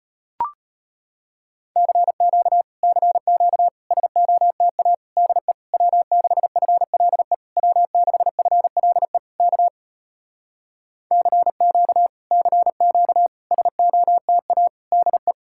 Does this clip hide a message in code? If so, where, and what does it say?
Yes: Morse code from 1.8 to 15.3 s, heard as a keyed tone, about 33 wpm, 700 Hz. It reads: CQ CQ SOTA DE W6FLE W6FLE K CQ CQ SOTA DE